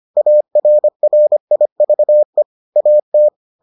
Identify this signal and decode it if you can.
Morse code, heard as a keyed tone, about 25 wpm, 600 Hz. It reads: ARRIVE AT